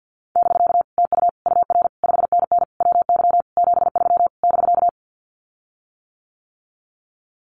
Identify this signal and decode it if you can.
Morse code, heard as a keyed tone, about 48 wpm, 700 Hz. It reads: <BK> TU UR 5NN WY 73 <BK>
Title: Common QSO Phrases Encore 50wpm